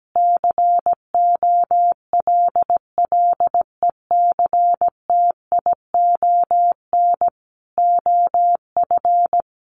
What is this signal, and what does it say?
Morse code, heard as a keyed tone, about 17 wpm, 700 Hz. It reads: COLLECTION OF
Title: Sentences from Top 1200 Words 17wpm